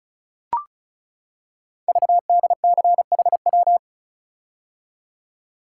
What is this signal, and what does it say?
Morse code, heard as a keyed tone, about 36 wpm, 700 Hz. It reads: VDCHW
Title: Code Groups - 5 Letters - 35wpm